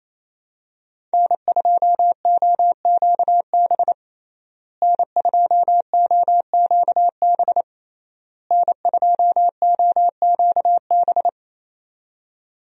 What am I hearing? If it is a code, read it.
Morse code, heard as a keyed tone, about 28 wpm, 700 Hz. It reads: N2OQB N2OQB N2OQB
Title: USA Call Signs - 5 Characters - Repeated 3x - 28wpm